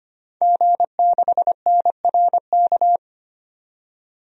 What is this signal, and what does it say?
Morse code, heard as a keyed tone, about 25 wpm, 700 Hz. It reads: G6NRK
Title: DXCC-100 Call Signs - 5 Characters - 25wpm